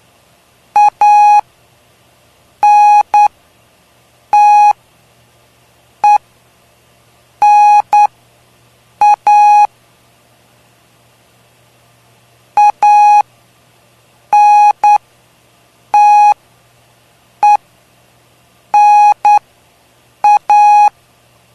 Morse code, heard as a keyed tone, about 9 wpm, spaced as for 6 wpm, 830 Hz. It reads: ANTENA ANTENA
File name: cw / cwzap / 03-PY2LCD-Li��o 3